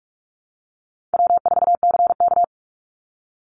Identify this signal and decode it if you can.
Morse code, heard as a keyed tone, about 46 wpm, 700 Hz. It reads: W4CK
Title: Easy Call Signs - 45wpm